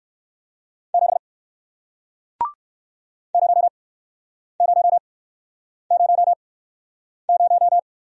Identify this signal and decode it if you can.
Morse code, heard as a keyed tone, about 60 wpm, spaced as for 9 wpm, 700 Hz. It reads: 7 0000